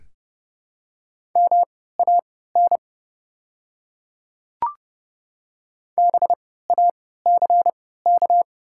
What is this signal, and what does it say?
Morse code, heard as a keyed tone, about 30 wpm, spaced as for 17 wpm, 700 Hz. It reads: MAN BACK
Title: 3x Character Spacing - Top 200 Words - 30wpm